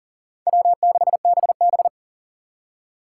Morse code, heard as a keyed tone, about 51 wpm, 700 Hz. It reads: W6BB